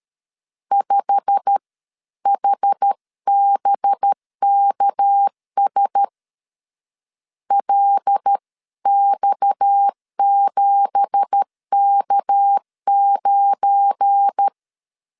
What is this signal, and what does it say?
Morse code, heard as a keyed tone, about 13 wpm, spaced as for 11 wpm, 800 Hz. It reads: 5HBKS LX7K9